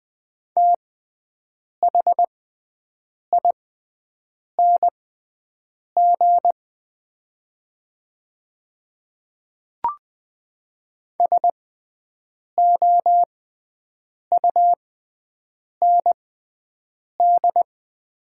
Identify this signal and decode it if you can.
Morse code, heard as a keyed tone, about 20 wpm, spaced as for 7 wpm, 690 Hz. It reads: THING SOUND
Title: 6x Character Spacing - Top 100 Words - 20wpm